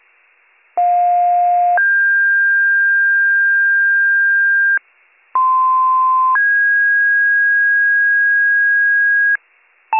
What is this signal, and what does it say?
Two-tone page, heard as a one-second tone, then a three-second tone, about 710 then 1600 Hz, about 1040 then 1650 Hz. It is heard from 0.8 to 9.4 s.